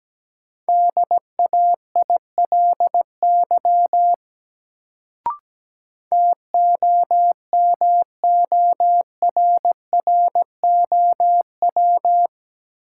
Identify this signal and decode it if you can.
Morse code, heard as a keyed tone, about 17 wpm, 700 Hz. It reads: DAILY TOMORROW
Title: Sentences from Top 1100 Words - New Word Drill - 17wpm